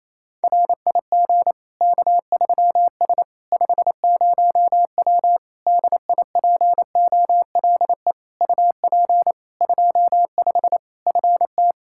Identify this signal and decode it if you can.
Morse code, heard as a keyed tone, about 28 wpm, 700 Hz. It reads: RIG K3S 50W DIPOLE UP 25 FT
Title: Practice QSO - Exchange 2 - Info - 28wpm